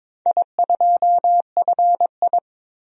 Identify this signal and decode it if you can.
Morse code, heard as a keyed tone, about 22 wpm, 700 Hz. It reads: I2FI
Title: DXCC-100 Call Signs - 4 Characters - 22wpm